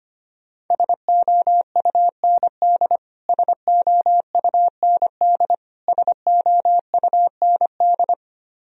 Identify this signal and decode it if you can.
Morse code, heard as a keyed tone, about 25 wpm, 700 Hz. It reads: SOUND SOUND SOUND
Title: Top 100 Words Repeated 25wpm